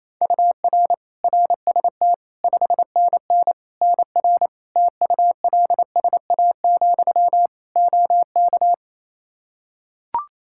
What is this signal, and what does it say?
Morse code, heard as a keyed tone, about 28 wpm, 700 Hz. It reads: UR RST 5NN NR TULSA, OK
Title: Practice QSO - Exchange 1 - Beginning - 28wpm